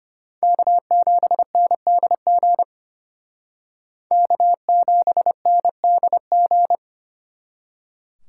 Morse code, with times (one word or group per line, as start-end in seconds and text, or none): 0.4-2.6 K7NDG
4.1-6.8 K7NDG